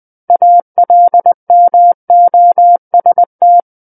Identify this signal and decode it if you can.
Morse code, heard as a keyed tone, about 20 wpm, 700 Hz. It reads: ALMOST
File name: 20 WPM